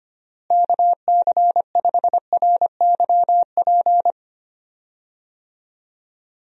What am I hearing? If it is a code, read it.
Morse code, heard as a keyed tone, about 25 wpm, 700 Hz. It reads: KC5RYP
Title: USA Call Signs - 6 Characters - Repeated 2x - 25wpm